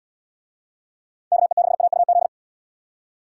Morse code, heard as a keyed tone, about 69 wpm, 700 Hz. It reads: KE6AUZ